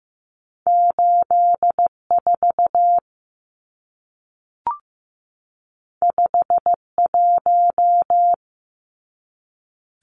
Morse code, heard as a keyed tone, about 15 wpm, 700 Hz. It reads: 84 51
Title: Numbers - 2-Digit 15wpm